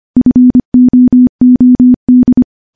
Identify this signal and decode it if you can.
Morse code, heard as a keyed tone, about 25 wpm, 250 Hz. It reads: FOOD
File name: morse / morse-training-25WPM